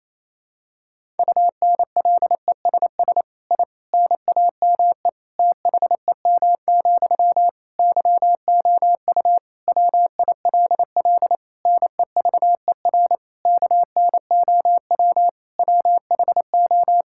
Morse code, heard as a keyed tone, about 28 wpm, 700 Hz. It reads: UNLESS I NAME THEM, YOU WILL NEVER KNOW WHO
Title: Sentences from Top 1000 Words - Review Part 6 - 28wpm